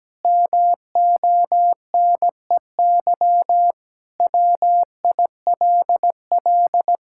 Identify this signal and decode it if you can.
Morse code, heard as a keyed tone, about 17 wpm, 700 Hz. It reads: MONEY WILL
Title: Sentences from Top 1000 Words - Review Part 5 - 17wpm